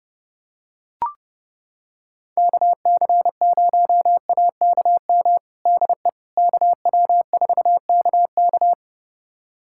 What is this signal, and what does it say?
Morse code, heard as a keyed tone, about 30 wpm, 700 Hz. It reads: KC0AKM DE KW4KK
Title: USA Callsign DE Callsign 30wpm